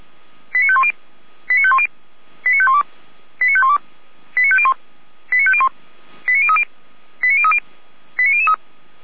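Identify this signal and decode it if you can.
Five-tone selective call (ZVEI), heard as quick runs of five tones, tones of about 70 ms. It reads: 89510 89510 89512 89512 89501 89501 89030 89030 89003
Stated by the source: Five-tone ZVEI selective call received on an SDR.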